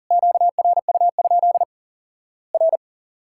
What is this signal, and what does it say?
Morse code, heard as a keyed tone, about 40 wpm, 700 Hz. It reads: QRU? R
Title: Worked All Europe DX Contest - QTC Exchange - 40wpm